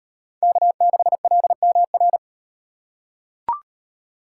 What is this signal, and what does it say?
Morse code, heard as a keyed tone, about 38 wpm, 700 Hz. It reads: K6LMR